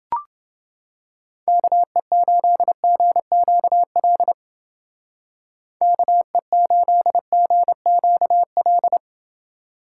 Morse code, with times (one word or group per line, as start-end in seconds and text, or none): 1.5-4.3 KE8GQL
5.8-9.0 KE8GQL